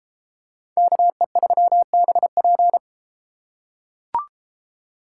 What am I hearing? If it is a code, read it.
Morse code, heard as a keyed tone, about 30 wpm, 700 Hz. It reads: KE3BP